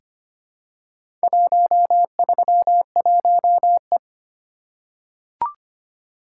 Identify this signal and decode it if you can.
Morse code, heard as a keyed tone, about 25 wpm, 700 Hz. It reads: 131E
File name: practice / 25wpm / Four Letter-Number Combinations - 25wpm